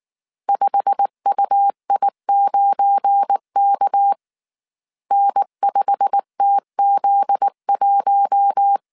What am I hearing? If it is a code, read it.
Morse code, heard as a keyed tone, about 19 wpm, 800 Hz. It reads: 5UI9K N5TZ1
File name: CW / DET / 1 9 1500C S A 18wpm 2459